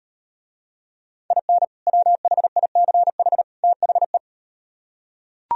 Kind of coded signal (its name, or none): Morse code